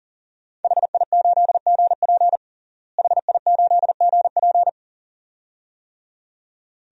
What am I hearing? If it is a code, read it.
Morse code, heard as a keyed tone, about 37 wpm, 700 Hz. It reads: HI8GP HI8GP
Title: DXCC-100 Call Signs - 5 Characters - Repeated 2x - 40wpm